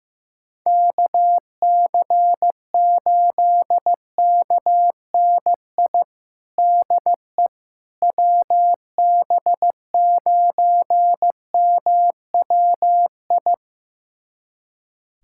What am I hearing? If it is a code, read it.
Morse code, heard as a keyed tone, about 15 wpm, 700 Hz. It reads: KC8KNI DE WB9MWI
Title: USA Callsign DE Callsign 15wpm